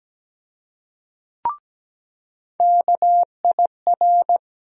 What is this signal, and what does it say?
Morse code, heard as a keyed tone, about 17 wpm, 700 Hz. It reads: KIR
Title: USA Call Sign Postfixes - 3 Letters 17wpm